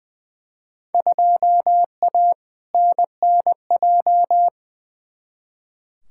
Morse code, heard as a keyed tone, about 20 wpm, 700 Hz. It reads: 2A NNJ
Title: Field Day Exchanges - 20wpm